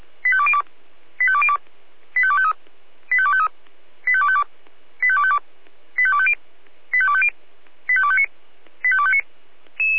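Five-tone selective call (ZVEI), heard as quick runs of five tones, tones of about 70 ms. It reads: 85202 85202 85253 85253 85252 85252 85250 85250 85259 85259
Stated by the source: ZVEI five-tone selective call 85102 received on SDR.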